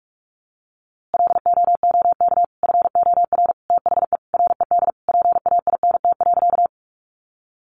Morse code, heard as a keyed tone, about 44 wpm, 700 Hz. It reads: LOOK FOR THE RED PAINT.
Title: A Sound Of Thunder 45wpm